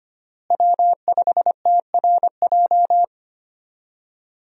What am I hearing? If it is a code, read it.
Morse code, heard as a keyed tone, about 25 wpm, 700 Hz. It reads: W5TRJ
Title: USA Call Signs - 5 Characters - Repeated 3x - 25wpm